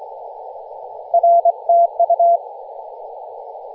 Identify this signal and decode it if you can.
Morse code, heard as a keyed tone, about 23 wpm, 680 Hz. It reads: R TU
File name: posts / 2016-nov-13-0348z-t30jy-28-034mhz-cw